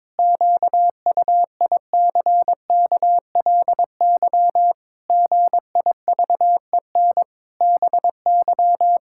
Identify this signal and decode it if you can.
Morse code, heard as a keyed tone, about 22 wpm, 700 Hz. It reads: QUICKLY GIVEN BY
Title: Pangram Challenge v2 - 22wpm